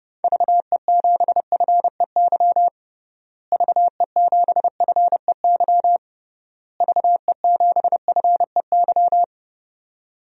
Morse code, heard as a keyed tone, about 30 wpm, 700 Hz. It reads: VE7FEY VE7FEY VE7FEY
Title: DXCC-100 Call Signs - 6 Characters - Repeated 3x - 30wpm